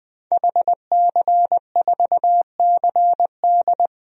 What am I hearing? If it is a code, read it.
Morse code, heard as a keyed tone, about 21 wpm, 700 Hz. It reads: HC4CD
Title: DXCC-100 Call Signs - 5 Characters - 20wpm